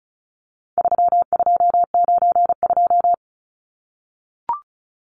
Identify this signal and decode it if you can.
Morse code, heard as a keyed tone, about 34 wpm, 700 Hz. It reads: 3292
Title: Numbers - 4-Digit 35wpm